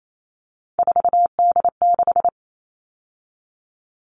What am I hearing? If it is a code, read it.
Morse code, heard as a keyed tone, about 28 wpm, 700 Hz. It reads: 4D6
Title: All Cut Numbers - 3-Digit 28wpm